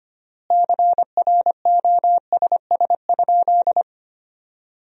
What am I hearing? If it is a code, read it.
Morse code, heard as a keyed tone, about 25 wpm, 700 Hz. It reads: CROSS?
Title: 3x Word-Spacing - Sentences from Top 300 Words - 25wpm